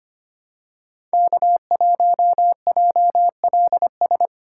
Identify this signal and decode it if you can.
Morse code, heard as a keyed tone, about 25 wpm, 700 Hz. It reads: K1JLS
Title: USA Call Signs - 5 Characters - 3x Character Spacing - 25wpm